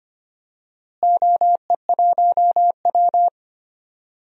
Morse code, heard as a keyed tone, about 25 wpm, 700 Hz. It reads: OE1W
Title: DXCC-100 Call Signs - 4 Characters - Repeated 2x - 25wpm